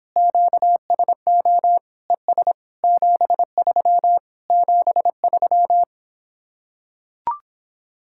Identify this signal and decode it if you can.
Morse code, heard as a keyed tone, about 26 wpm, 700 Hz. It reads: QSO ES 73 73